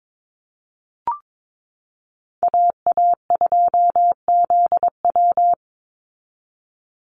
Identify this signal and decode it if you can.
Morse code, heard as a keyed tone, about 22 wpm, 700 Hz. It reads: AA2ZW
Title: Call Signs Repeated 2x - 22wpm